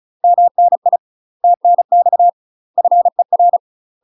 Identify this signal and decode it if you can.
Morse code, heard as a keyed tone, about 36 wpm, 700 Hz. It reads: MNI TNX FER
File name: UK MASTER QSOs TRAINING / 35WPM / UK MASTER QSOs 0010